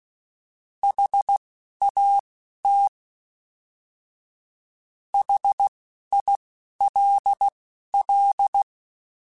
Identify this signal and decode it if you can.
Morse code, heard as a keyed tone, about 16 wpm, spaced as for 11 wpm, 800 Hz. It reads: HAT HILL